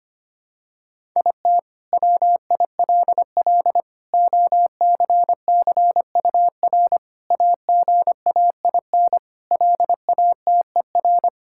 Morse code, heard as a keyed tone, about 25 wpm, 700 Hz. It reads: IT WILL OCCUR AGAIN LATER
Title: Sentences from Top 1000 Words - Review Part 6 - 25wpm